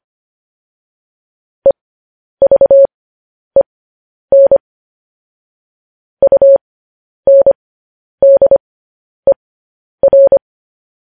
Morse code, heard as a keyed tone, about 25 wpm, spaced as for 10 wpm, 550 Hz. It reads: EVEN UNDER